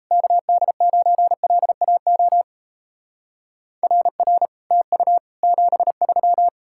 Morse code, heard as a keyed tone, about 34 wpm, 700 Hz. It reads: KD9LAO RR TU 73
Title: Speed-Racing Common QSO Phrases - 28wpm